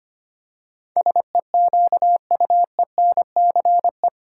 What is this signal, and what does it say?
Morse code, heard as a keyed tone, about 25 wpm, 700 Hz. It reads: SEQUENCE
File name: Eight Letter Words - 25wpm